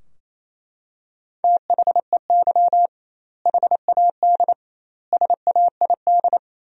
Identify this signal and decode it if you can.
Morse code, heard as a keyed tone, about 28 wpm, 700 Hz. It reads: THEY HAD SAID
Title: 2x Word-Spacing - Sentences from Top 400 Words - 28wpm